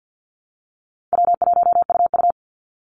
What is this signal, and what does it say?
Morse code, heard as a keyed tone, about 48 wpm, 700 Hz. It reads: W1UU
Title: Easy Call Signs Repeated 3x - 50wpm